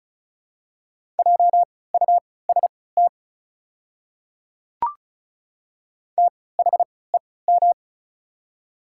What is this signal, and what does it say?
Morse code, heard as a keyed tone, about 36 wpm, spaced as for 20 wpm, 700 Hz. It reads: JUST THEM
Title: 3x Character Spacing - Top 200 Words - 35wpm